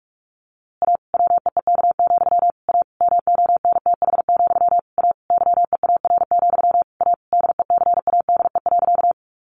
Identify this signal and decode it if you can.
Morse code, heard as a keyed tone, about 45 wpm, 700 Hz. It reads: A WEEK, A MONTH, A YEAR, A DECADE.
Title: A Sound Of Thunder 45wpm